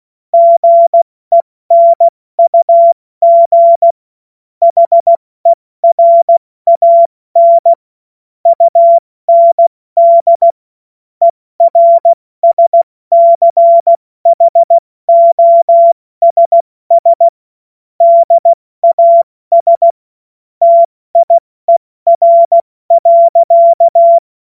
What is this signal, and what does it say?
Morse code, heard as a keyed tone, about 16 wpm, 680 Hz. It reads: GENUG HERAN UND ERSCHOSS DAS TIER.
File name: Morsekurs / Witze / Witz 07. März 2023 70 F80